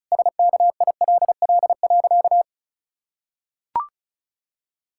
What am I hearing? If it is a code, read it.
Morse code, heard as a keyed tone, about 36 wpm, 700 Hz. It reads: SKILL.
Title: Pangram Challenge v2 - 35wpm